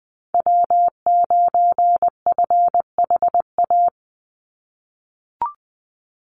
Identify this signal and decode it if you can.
Morse code, heard as a keyed tone, about 20 wpm, 700 Hz. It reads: W9FHA
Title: NA SOTA Chaser Callsigns 20wpm